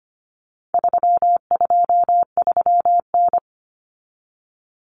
Morse code, heard as a keyed tone, about 25 wpm, 700 Hz. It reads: 323N
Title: Common Cut Numbers - 4-Digit 25wpm